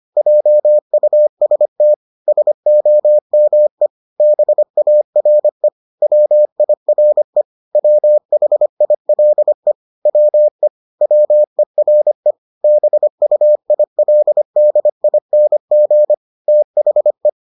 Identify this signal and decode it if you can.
Morse code, heard as a keyed tone, about 25 wpm, 600 Hz. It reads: JUST SOME BARE WIRE WHILE WE WERE BUILDING THE